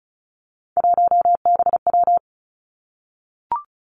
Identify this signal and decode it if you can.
Morse code, heard as a keyed tone, about 36 wpm, 700 Hz. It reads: 1BW
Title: Three Challenging Letter-Number Combinations 35wpm